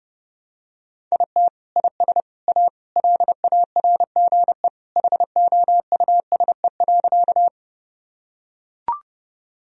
Morse code, heard as a keyed tone, about 30 wpm, 700 Hz. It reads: IT IS A LARGE HOUSE.
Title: Sentences from Top 1000 Words - Review Part 1 - 30wpm